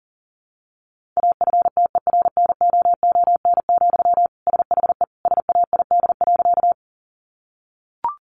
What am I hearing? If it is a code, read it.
Morse code, heard as a keyed tone, about 40 wpm, 700 Hz. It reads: AFTERNOON, SHE SAID.
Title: Sentences from Top 1500 Words 40wpm